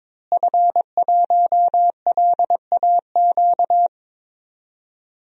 Morse code, heard as a keyed tone, about 22 wpm, 700 Hz. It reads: F1LAQ
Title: DXCC-100 Call Signs - 5 Characters - Repeated 2x - 22wpm